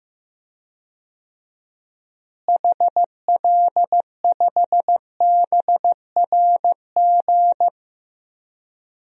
Morse code, heard as a keyed tone, about 15 wpm, 700 Hz. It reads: HL5BRG
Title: DXCC-100 Call Signs - 6 Characters - Repeated 3x - 15wpm